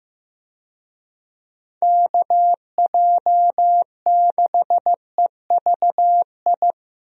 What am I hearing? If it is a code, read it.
Morse code, heard as a keyed tone, about 15 wpm, 700 Hz. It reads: KJ6EVI
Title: USA Call Signs - 6 Characters - 15wpm